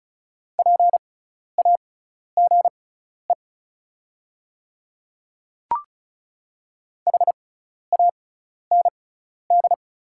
Morse code, heard as a keyed tone, about 34 wpm, spaced as for 12 wpm, 700 Hz. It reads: PAGE HAND
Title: 6x Character Spacing - Top 300 Words - 35wpm